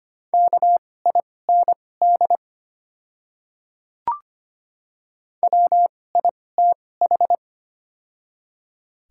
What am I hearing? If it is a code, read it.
Morse code, heard as a keyed tone, about 25 wpm, spaced as for 18 wpm, 700 Hz. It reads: KIND WITH
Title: 2x Character Spacing - Top 200 Words - 25wpm